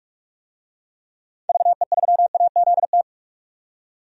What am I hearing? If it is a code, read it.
Morse code, heard as a keyed tone, about 41 wpm, 700 Hz. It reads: VE3AZT
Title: DXCC-100 Call Signs - 6 Characters - 45wpm